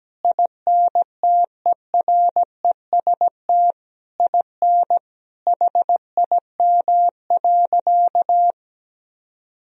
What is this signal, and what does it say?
Morse code, heard as a keyed tone, about 17 wpm, 700 Hz. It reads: INTEREST IN HIM.
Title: Sentences from Top 1000 Words - Review Part 3 - 17wpm